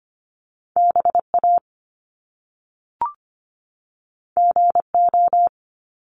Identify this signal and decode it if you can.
Morse code, heard as a keyed tone, about 25 wpm, 700 Hz. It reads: BA GO
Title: Top Two Letter Combinations in Words 25wpm